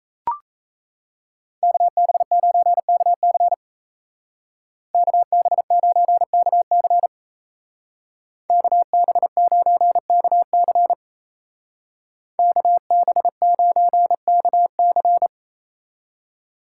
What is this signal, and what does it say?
Morse code, heard as a keyed tone, about 36 wpm, 700 Hz. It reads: KB9KC KB9KC KB9KC KB9KC